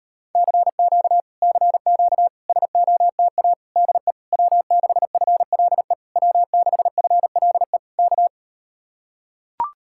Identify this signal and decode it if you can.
Morse code, heard as a keyed tone, about 38 wpm, 700 Hz. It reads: CQ CQ SOTA DE W6FLE W6FLE K